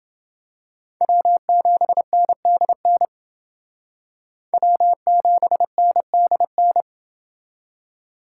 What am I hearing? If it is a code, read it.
Morse code, heard as a keyed tone, about 28 wpm, 700 Hz. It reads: W7NDN W7NDN